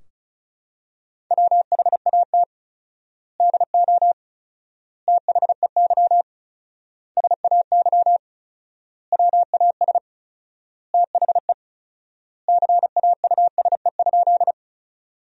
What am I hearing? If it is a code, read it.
Morse code, heard as a keyed tone, about 36 wpm, 700 Hz. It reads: WHAT DO THEY SAY WAS THE CAUSE?
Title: 4x Word-Spacing - Sentences from Top 200 Words - 35wpm